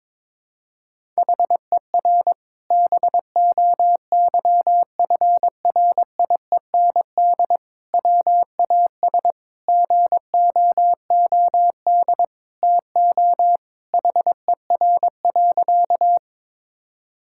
Morse code, heard as a keyed tone, about 22 wpm, 700 Hz. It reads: HER BOYFRIEND WAS GOOD TO HER.
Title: Sentences from Top 1000 Words - Review Part 5 - 22wpm